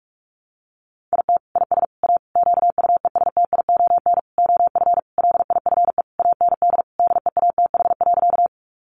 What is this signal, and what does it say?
Morse code, heard as a keyed tone, about 44 wpm, 700 Hz. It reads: IT IS A QUESTION OF LIFE AND DEATH.
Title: Sentences from Top 1100 Words 45wpm